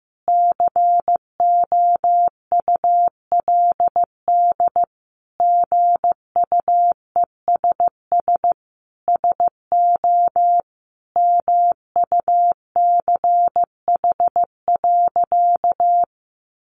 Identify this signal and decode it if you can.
Morse code, heard as a keyed tone, about 14 wpm, 700 Hz. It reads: COULD GUESS SO MUCH.
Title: Sentences from Top 1500 Words 15wpm